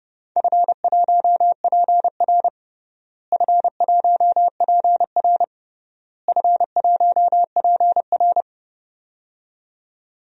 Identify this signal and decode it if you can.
Morse code, heard as a keyed tone, about 30 wpm, 700 Hz. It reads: F1PR F1PR F1PR
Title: DXCC-100 Call Signs - 4 Characters - Repeated 3x - 30wpm